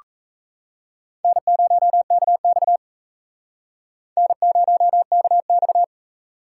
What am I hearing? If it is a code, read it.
Morse code, heard as a keyed tone, about 35 wpm, 700 Hz. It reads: N0KX N0KX
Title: Speed Racing - USA 1x2 Call Signs 28wpm